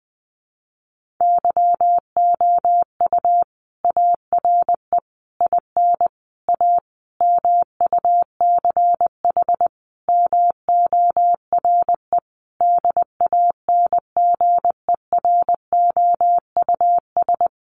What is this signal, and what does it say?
Morse code, heard as a keyed tone, about 20 wpm, 700 Hz. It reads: YOU ARE IN A MUCH MORE DANGEROUS
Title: Sentences from Top 1400 Words 20wpm